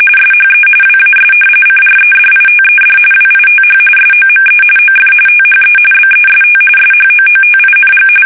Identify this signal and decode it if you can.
RTTY, about 45.45 baud. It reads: SHE WAS CREDITED WITH SOME OF THE BEST SHOOTING
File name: fsk850 / battleshipnj